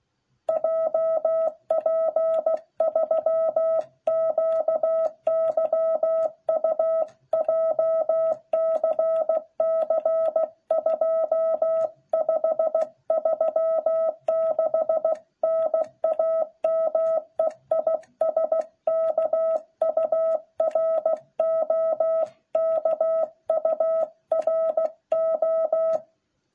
Morse code, heard as a keyed tone, about 16 wpm, 650 Hz. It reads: JP3QYU JCC 2536 NAME IS KURO KURO